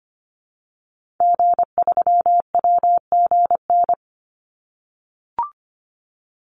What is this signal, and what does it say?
Morse code, heard as a keyed tone, about 25 wpm, 700 Hz. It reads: G3WGN
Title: Call Signs - 25wpm